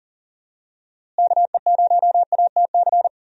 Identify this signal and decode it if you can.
Morse code, heard as a keyed tone, about 40 wpm, 700 Hz. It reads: KE0ATC
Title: USA Call Signs - 6 Characters - Repeated 3x - 40wpm